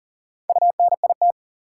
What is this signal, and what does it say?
Morse code, heard as a keyed tone, about 40 wpm, 700 Hz. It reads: UNIT
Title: Sentences from Top 1000 Words - Review Part 2 - New Word Drill - 40wpm